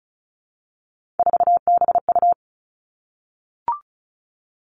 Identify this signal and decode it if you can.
Morse code, heard as a keyed tone, about 36 wpm, 700 Hz. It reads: 4BU
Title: Three Challenging Letter-Number Combinations 35wpm